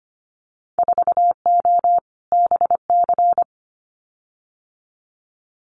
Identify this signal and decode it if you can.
Morse code, heard as a keyed tone, about 25 wpm, 700 Hz. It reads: 4O BC
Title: Winter Field Day Exchanges Encore 25wpm